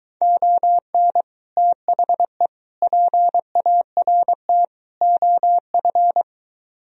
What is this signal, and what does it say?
Morse code, heard as a keyed tone, about 24 wpm, 700 Hz. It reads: ON THE PART OF